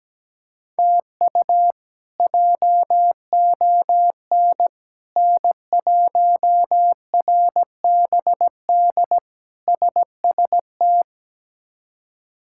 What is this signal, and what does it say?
Morse code, heard as a keyed tone, about 17 wpm, 700 Hz. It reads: TU JON N1RBD SST
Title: K1USN SST QSO Practice - 17wpm